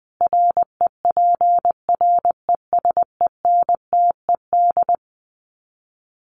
Morse code, heard as a keyed tone, about 20 wpm, 700 Hz. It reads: REPRESENTED
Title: Sets of 5 Words 20wpm